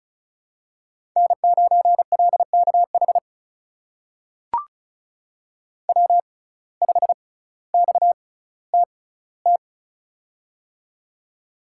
Morse code, heard as a keyed tone, about 36 wpm, spaced as for 18 wpm, 700 Hz. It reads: N9LKH W5XTT
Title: USA Call Signs - 5 Characters - 6x Character Spacing - 35wpm